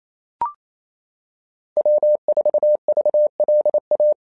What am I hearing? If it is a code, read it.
Morse code, heard as a keyed tone, about 28 wpm, 600 Hz. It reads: W4VLA
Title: SOTA QSO Practice 28wpm